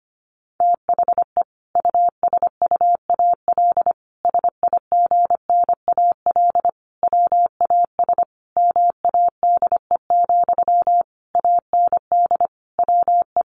Morse code, heard as a keyed tone, about 25 wpm, 700 Hz. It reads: THE USUAL SIGNAL WAS MADE, AND WE
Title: Sentences from Top 1400 Words 25wpm